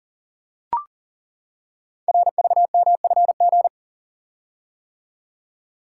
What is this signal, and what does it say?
Morse code, heard as a keyed tone, about 38 wpm, 700 Hz. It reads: RVMFG
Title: Code Groups - 5 Letter:Numbers - 40wpm